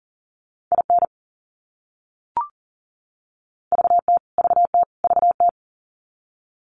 Morse code, heard as a keyed tone, about 40 wpm, 700 Hz. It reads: IN VT VT VT
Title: US States Repeated 3x 40wpm